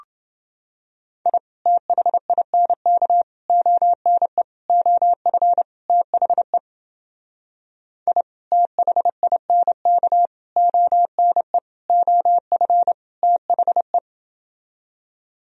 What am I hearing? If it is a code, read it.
Morse code, heard as a keyed tone, about 30 wpm, 700 Hz. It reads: I THINK ONE OF THE I THINK ONE OF THE